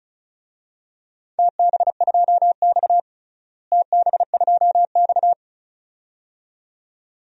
Morse code, heard as a keyed tone, about 36 wpm, 700 Hz. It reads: TB2X TB2X
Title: DXCC-100 Call Signs - 4 Characters - Repeated 2x - 35wpm